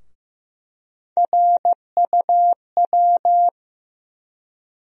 Morse code, heard as a keyed tone, about 15 wpm, 700 Hz. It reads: RUW
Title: USA Call Sign Postfixes - 3 Letters 15wpm